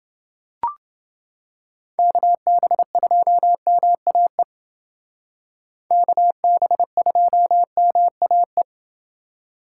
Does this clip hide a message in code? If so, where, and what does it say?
Yes: Morse code from 2.0 to 8.6 s, heard as a keyed tone, about 28 wpm, 700 Hz. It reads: KB2MAE KB2MAE